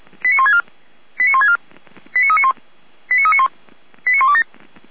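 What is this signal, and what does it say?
Five-tone selective call (ZVEI), heard as quick runs of five tones, tones of about 70 ms. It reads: 89164 89164 89391 89391 89127
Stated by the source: ZVEI five-tone selective call received by SDR, recorded on the audio server of Feuerwehr Orlamünde (f85360).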